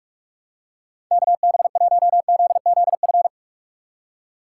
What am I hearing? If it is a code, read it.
Morse code, heard as a keyed tone, about 46 wpm, 700 Hz. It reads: KB1ZZF
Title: USA Call Signs - 6 Characters - 6x Character Spacing - 45wpm